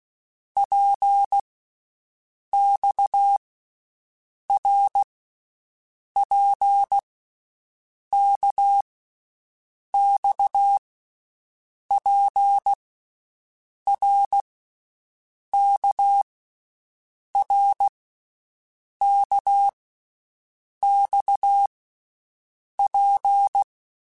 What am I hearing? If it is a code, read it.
Morse code, heard as a keyed tone, about 16 wpm, 800 Hz. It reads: P X R P K X P R K R K X P
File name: Basic Letter Groups By7Groups / RKPX / RKPX@13